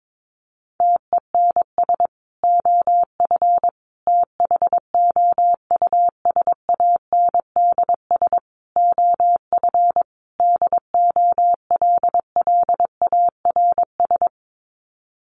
Morse code, heard as a keyed tone, about 22 wpm, 700 Hz. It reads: TENS OF THOUSANDS OF DOLLARS
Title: Sets of 5 Words Encore2 22wpm